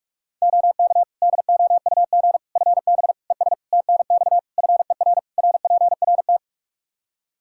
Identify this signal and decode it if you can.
Morse code, heard as a keyed tone, about 46 wpm, 700 Hz. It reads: OK DOUG FB ES TNX FER RPRT
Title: Practice QSO - Exchange 2 - Info - 45wpm